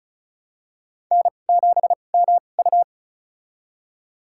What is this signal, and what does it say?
Morse code, heard as a keyed tone, about 36 wpm, spaced as for 25 wpm, 700 Hz. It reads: N7MU
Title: USA Call Signs - 2x Character Spacing - 35wpm